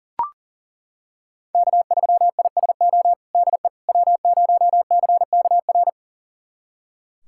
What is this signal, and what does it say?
Morse code, heard as a keyed tone, about 40 wpm, 700 Hz. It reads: K3ISO DE W0CKR
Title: USA Callsign DE Callsign - 5 Characters - 40wpm